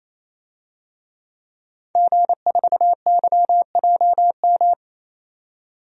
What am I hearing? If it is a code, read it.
Morse code, heard as a keyed tone, about 28 wpm, 700 Hz. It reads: G4YJM
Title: DXCC-100 Call Signs - 5 Characters - 28wpm